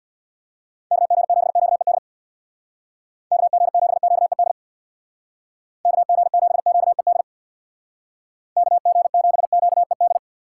Morse code, heard as a keyed tone, about 65 wpm, 700 Hz. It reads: KG7QED KG7QED KG7QED KG7QED